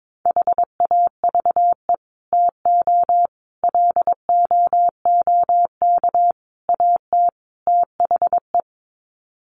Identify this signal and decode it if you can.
Morse code, heard as a keyed tone, about 22 wpm, 700 Hz. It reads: HAVE TO LOOK AT THE
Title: Sets of 5 Words Encore 22wpm